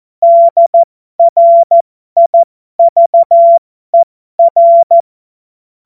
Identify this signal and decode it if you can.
Morse code, heard as a keyed tone, about 14 wpm, spaced as for 12 wpm, 680 Hz. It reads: DRIVER